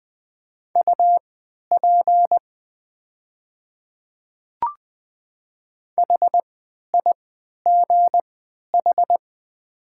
Morse code, heard as a keyed tone, about 20 wpm, spaced as for 11 wpm, 700 Hz. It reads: UP HIGH
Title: 3x Character Spacing - Top 200 Words - 20wpm